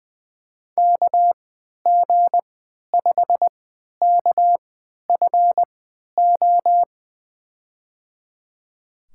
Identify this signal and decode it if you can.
Morse code, heard as a keyed tone, about 20 wpm, spaced as for 11 wpm, 700 Hz. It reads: KG5KFO
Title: USA Call Signs - 6 Characters - 3x Character Spacing - 20wpm